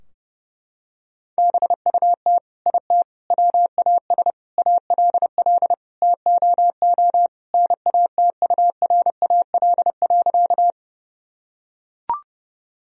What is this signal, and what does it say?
Morse code, heard as a keyed tone, about 30 wpm, 700 Hz. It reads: BUT IT WAS ALL TOO NATURAL.
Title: Sentences from Top 1000 Words - Review Part 4 - 30wpm